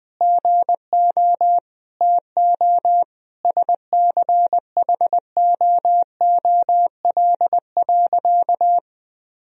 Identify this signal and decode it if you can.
Morse code, heard as a keyed tone, about 20 wpm, 700 Hz. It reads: GO TO SCHOOL.
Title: Sentences from Top 1000 Words - Review Part 4 - 20wpm